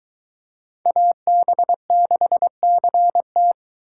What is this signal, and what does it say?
Morse code, heard as a keyed tone, about 22 wpm, 700 Hz. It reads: AB6CT